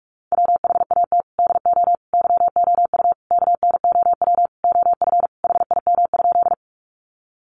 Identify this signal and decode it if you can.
Morse code, heard as a keyed tone, about 44 wpm, 700 Hz. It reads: WHAT DO YOU KNOW OF HIM?
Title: Sentences from Top 100 Words 45wpm